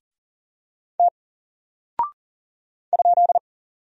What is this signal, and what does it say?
Morse code, heard as a keyed tone, about 40 wpm, 700 Hz. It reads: T ?
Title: Single Character (Letter:Number:Punctuation:Prosign) Alt 40wpm